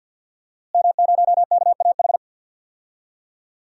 Morse code, heard as a keyed tone, about 52 wpm, 700 Hz. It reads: M0KAH
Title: DXCC-100 Call Signs - 5 Characters - Repeated 3x - 50wpm